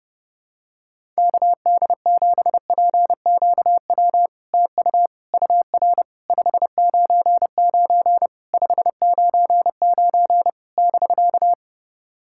Morse code, heard as a keyed tone, about 30 wpm, 700 Hz. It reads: KD7PQW TU UR 599 599 <BK>
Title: SOTA QSO Practice 30wpm